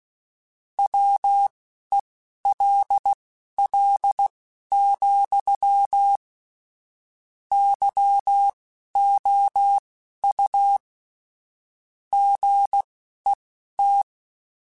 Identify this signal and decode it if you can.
Morse code, heard as a keyed tone, about 16 wpm, spaced as for 11 wpm, 800 Hz. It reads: WELL, YOU GET
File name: Technical Subjects / Technical Subjects-mp3s / AND Gate @10